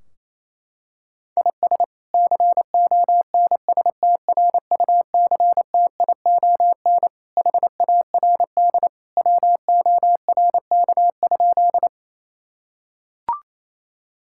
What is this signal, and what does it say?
Morse code, heard as a keyed tone, about 27 wpm, 700 Hz. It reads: IS CONSTRUCTION HARD WORK?
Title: Sentences from Top 1600 Words 28wpm